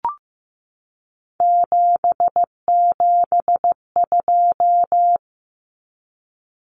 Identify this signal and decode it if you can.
Morse code, heard as a keyed tone, about 15 wpm, 700 Hz. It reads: 772
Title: Numbers - 3-Digit 15wpm